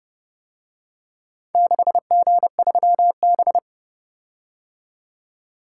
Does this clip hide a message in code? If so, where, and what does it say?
Yes: Morse code from 1.6 to 3.6 s, heard as a keyed tone, about 30 wpm, 700 Hz. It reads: 6G3B